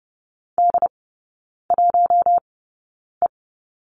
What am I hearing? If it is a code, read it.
Morse code, heard as a keyed tone, about 30 wpm, spaced as for 9 wpm, 700 Hz. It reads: D1E
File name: 081 Lesson 14e - Number 2 - Three Characters 30wpm